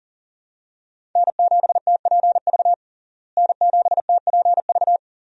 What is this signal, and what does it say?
Morse code, heard as a keyed tone, about 40 wpm, 700 Hz. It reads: N7TPV N7TPV
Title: USA Call Signs - 5 Characters - Repeated 2x - 40wpm